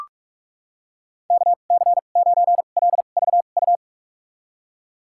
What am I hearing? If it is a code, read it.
Morse code, heard as a keyed tone, about 46 wpm, spaced as for 32 wpm, 700 Hz. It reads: KC9LVU
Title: USA Call Signs - 6 Characters - 2x Character Spacing - 45wpm